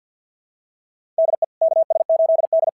Morse code, heard as a keyed tone, about 50 wpm, 650 Hz. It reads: DE KS8D
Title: SOTA QSO Practice 50wpm